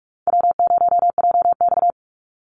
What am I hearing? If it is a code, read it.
Morse code, heard as a keyed tone, about 44 wpm, 700 Hz. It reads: W0JX
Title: Call Signs Repeated 2x - 45wpm